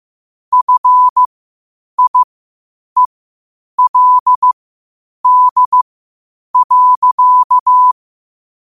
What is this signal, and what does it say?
Morse code, heard as a keyed tone, about 15 wpm, spaced as for 8 wpm, 1000 Hz. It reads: FIELD.
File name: qotdfiles / 8wpm